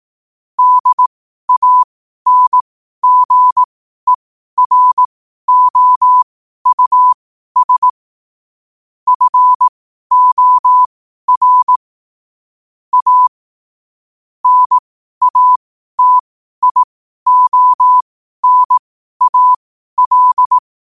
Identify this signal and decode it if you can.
Morse code, heard as a keyed tone, about 18 wpm, spaced as for 12 wpm, 1000 Hz. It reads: DANGEROUS FOR A NATIONAL
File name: qotdfiles / 12wpm